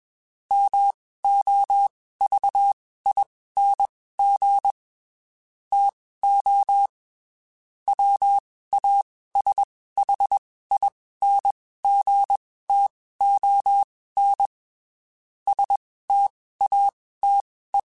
Morse code, heard as a keyed tone, about 21 wpm, spaced as for 15 wpm, 800 Hz. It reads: MOVING TO WASHINGTON STATE
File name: (spaced) / QSOs Spaced-mp3s / WD0BGP 1 @20 sp